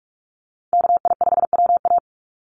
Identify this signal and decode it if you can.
Morse code, heard as a keyed tone, about 44 wpm, 700 Hz. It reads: KI5WA
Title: NA SOTA Chaser Callsigns 45wpm